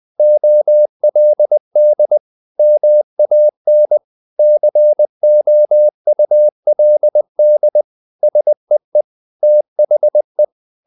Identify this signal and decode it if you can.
Morse code, heard as a keyed tone, about 20 wpm, 600 Hz. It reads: OLD MAN COULD SEE THE
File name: the old man and the sea / the old man and the sea part 1 20wpm